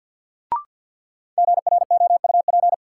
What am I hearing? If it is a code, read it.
Morse code, heard as a keyed tone, about 51 wpm, 700 Hz. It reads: GROUP
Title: Sentences from Top 1000 Words - Review Part 1 - New Word Drill - 50wpm